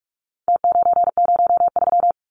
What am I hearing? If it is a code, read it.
Morse code, heard as a keyed tone, about 44 wpm, 700 Hz. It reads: T903
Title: Numbers - Contest Style - 4-Digit 45wpm